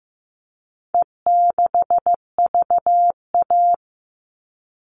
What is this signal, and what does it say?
Morse code, heard as a keyed tone, about 15 wpm, 700 Hz. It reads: E6VA
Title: All Cut Numbers - 4-Digit 15wpm